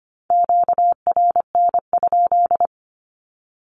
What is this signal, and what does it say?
Morse code, heard as a keyed tone, about 25 wpm, 700 Hz. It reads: QRN?
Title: Q-Codes with Spoken Meaning - 25wpm